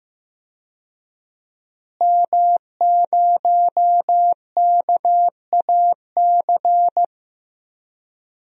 Morse code, heard as a keyed tone, about 15 wpm, 700 Hz. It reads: M0KAC
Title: DXCC-100 Call Signs - 5 Characters - Repeated 3x - 15wpm